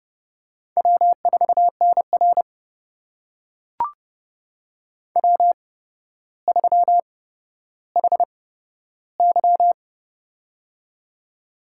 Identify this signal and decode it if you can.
Morse code, heard as a keyed tone, about 30 wpm, spaced as for 13 wpm, 700 Hz. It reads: W4NR W3HY
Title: USA Call Signs - 4 Characters - 8x Character Spacing - 30wpm